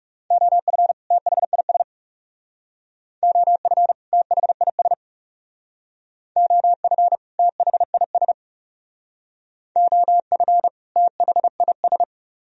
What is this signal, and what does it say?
Morse code, heard as a keyed tone, about 38 wpm, 700 Hz. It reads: OF THIS OF THIS OF THIS OF THIS